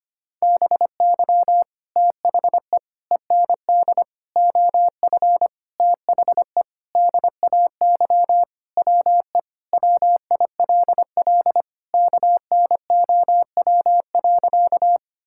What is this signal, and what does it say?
Morse code, heard as a keyed tone, about 25 wpm, 700 Hz. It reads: BY THE END OF THE DAY WE WILL KNOW.
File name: practice / 25wpm / Sentences with 4-Word Phrases - 25wpm